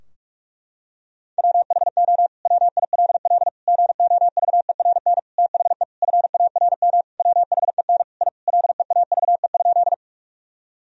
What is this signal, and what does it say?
Morse code, heard as a keyed tone, about 46 wpm, 700 Hz. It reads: WHO WILL GOVERN THE FARM WHEN I LEAVE?
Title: Sentences from Top 1000 Words - Review Part 2 - 45wpm